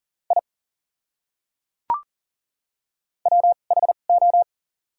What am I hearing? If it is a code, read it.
Morse code, heard as a keyed tone, about 40 wpm, spaced as for 29 wpm, 700 Hz. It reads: I WHO
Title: 2x Character Spacing - Top 200 Words - 40wpm